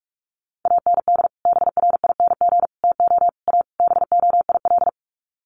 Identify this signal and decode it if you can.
Morse code, heard as a keyed tone, about 44 wpm, 700 Hz. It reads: AND BRING TO A BOIL
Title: Sets of 5 Words Encore 45wpm